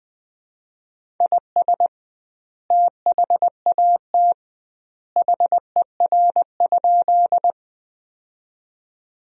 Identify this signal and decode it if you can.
Morse code, heard as a keyed tone, about 20 wpm, 700 Hz. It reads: IS THAT HER?
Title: 2x Word-Spacing - Sentences from Top 100 Words - 20wpm